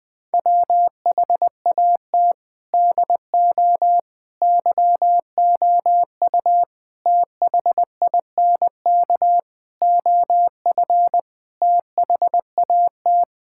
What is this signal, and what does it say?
Morse code, heard as a keyed tone, about 20 wpm, 700 Hz. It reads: WHAT DO YOU THINK OF THAT
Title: Sentences from Top 1000 Words - Review Part 6 - 20wpm